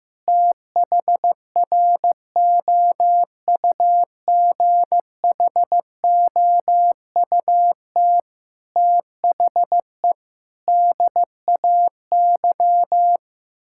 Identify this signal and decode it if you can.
Morse code, heard as a keyed tone, about 16 wpm, 700 Hz. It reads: THROUGHOUT THE DAY
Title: Sentences from Top 1000 Words - Review Part 6 - 15wpm